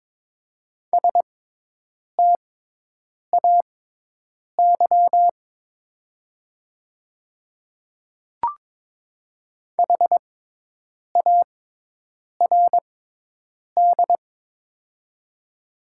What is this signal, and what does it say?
Morse code, heard as a keyed tone, about 22 wpm, spaced as for 8 wpm, 700 Hz. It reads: STAY HARD
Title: 6x Character Spacing - Top 500 Words - 22wpm